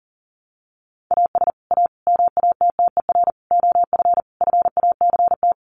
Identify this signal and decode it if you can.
Morse code, heard as a keyed tone, about 37 wpm, 700 Hz. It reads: AS A MATTER OF FACT
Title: Sets of 5 Words 40wpm